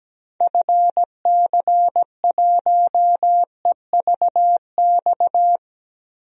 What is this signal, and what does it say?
Morse code, heard as a keyed tone, about 17 wpm, 700 Hz. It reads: FC1EVX
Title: DXCC-100 Call Signs - 6 Characters - Repeated 3x - 17wpm